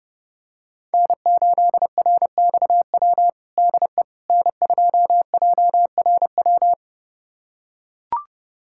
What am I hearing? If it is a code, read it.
Morse code, heard as a keyed tone, about 30 wpm, 700 Hz. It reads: N8RXW DE N2JRW
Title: USA Callsign DE Callsign - 5 Characters - 30wpm